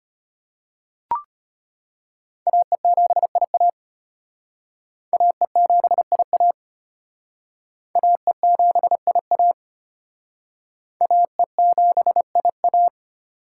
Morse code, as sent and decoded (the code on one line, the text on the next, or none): .- . --... .. .- / .- . --... .. .- / .- . --... .. .- / .- . --... .. .-
AE7IA AE7IA AE7IA AE7IA